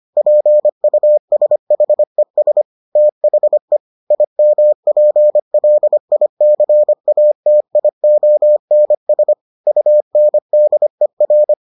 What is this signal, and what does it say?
Morse code, heard as a keyed tone, about 25 wpm, 600 Hz. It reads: PUSHES THE IMPLICATIONS UNDER